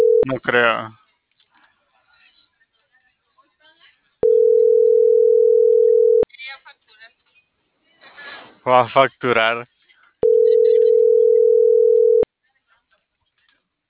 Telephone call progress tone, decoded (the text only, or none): ringback tone